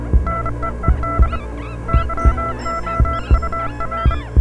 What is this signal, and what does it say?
Morse code, heard as a keyed tone, about 25 wpm, 1340 Hz, with noise. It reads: NEED YOUR